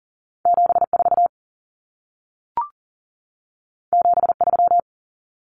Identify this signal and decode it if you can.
Morse code, heard as a keyed tone, about 40 wpm, 700 Hz. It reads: 74 73
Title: Common Cut Numbers - 2-Digit 40wpm